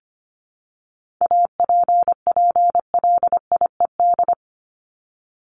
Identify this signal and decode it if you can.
Morse code, heard as a keyed tone, about 25 wpm, 700 Hz. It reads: APPLIED
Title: ED Suffix - Top 100 Words with Base Words 25wpm